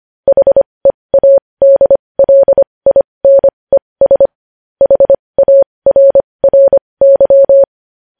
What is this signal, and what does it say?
Morse code, heard as a keyed tone, about 25 wpm, spaced as for 20 wpm, 550 Hz. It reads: HEADLINES HARRY
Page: news headlines in morse code at 20 wpm